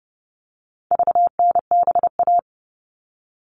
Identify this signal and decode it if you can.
Morse code, heard as a keyed tone, about 29 wpm, 700 Hz. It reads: VNBA
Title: All Cut Numbers - 4-Digit 30wpm